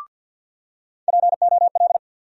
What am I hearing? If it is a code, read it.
Morse code, heard as a keyed tone, about 50 wpm, 700 Hz. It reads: POL
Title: USA Call Sign Postfixes - 3 Letters 50wpm